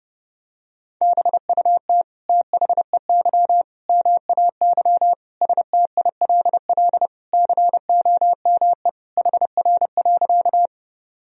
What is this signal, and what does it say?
Morse code, heard as a keyed tone, about 32 wpm, 700 Hz. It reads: BUT THEY MAY STILL COME HR.
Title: Sentences from Top 300 Words and Abbreviations 30wpm